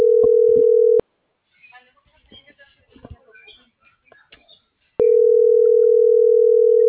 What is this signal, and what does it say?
Ringback tone (North America), 440 + 480 Hz.